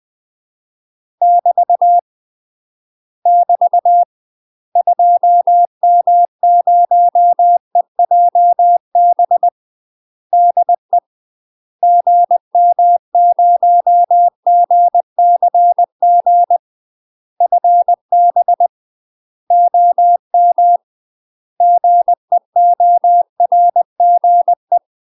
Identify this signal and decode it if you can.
Morse code, heard as a keyed tone, about 20 wpm, 700 Hz. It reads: <BT> <BT> 2M0EJB DE GM0GCG FB OM GEORGE
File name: UK MASTER QSOs TRAINING / 20WPM / UK MASTER QSOs 0018